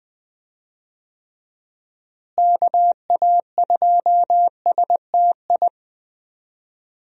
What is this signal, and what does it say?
Morse code, heard as a keyed tone, about 20 wpm, 700 Hz. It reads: KA2STI
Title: USA Call Signs - 6 Characters - Repeated 2x - 20wpm